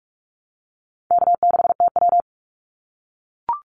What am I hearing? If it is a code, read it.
Morse code, heard as a keyed tone, about 44 wpm, 700 Hz. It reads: K6TW
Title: NA SOTA Activator Callsigns 45wpm